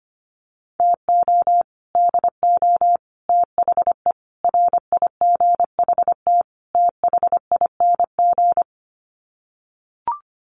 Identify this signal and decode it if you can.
Morse code, heard as a keyed tone, about 25 wpm, 700 Hz. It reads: TO DO THE RIGHT THING
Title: Sets of 5 Words 25wpm